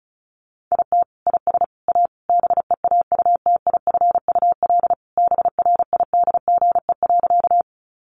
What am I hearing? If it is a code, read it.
Morse code, heard as a keyed tone, about 34 wpm, 700 Hz. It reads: IT IS A BEAUTIFUL BRIDGE.
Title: Sentences from Top 1200 Words 35wpm